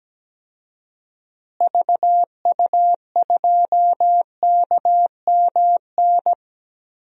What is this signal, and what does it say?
Morse code, heard as a keyed tone, about 17 wpm, 700 Hz. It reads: VU2KMN
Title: DXCC-100 Call Signs - 6 Characters - Repeated 2x - 17wpm